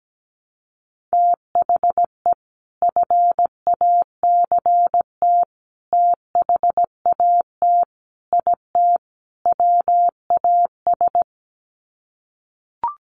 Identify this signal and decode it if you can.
Morse code, heard as a keyed tone, about 17 wpm, 700 Hz. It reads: THE FACT THAT IT WAS
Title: Sets of 5 Words Encore2 17wpm